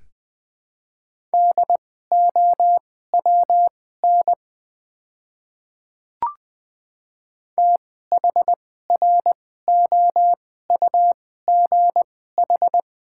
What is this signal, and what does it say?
Morse code, heard as a keyed tone, about 20 wpm, spaced as for 14 wpm, 700 Hz. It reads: DOWN THROUGH
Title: 2x Character Spacing - Top 300 Words - 20wpm